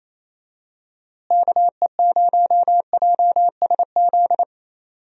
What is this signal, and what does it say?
Morse code, heard as a keyed tone, about 28 wpm, 700 Hz. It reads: KE0JSZ
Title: USA Call Signs - 6 Characters - Repeated 2x - 28wpm